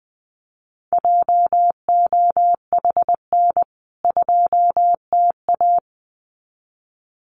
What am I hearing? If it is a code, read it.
Morse code, heard as a keyed tone, about 20 wpm, 700 Hz. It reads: JOHN 2TA
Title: CWT Exchange 20wpm